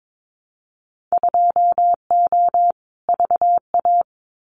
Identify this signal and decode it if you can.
Morse code, heard as a keyed tone, about 22 wpm, 700 Hz. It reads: 2O VA
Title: Winter Field Day Exchanges 22wpm